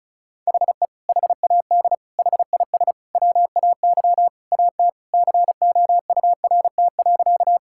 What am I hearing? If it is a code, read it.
Morse code, heard as a keyed tone, about 36 wpm, 700 Hz. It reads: HE HAD HIS WAY AT COURT.
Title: Sentences from Top 1600 Words 35wpm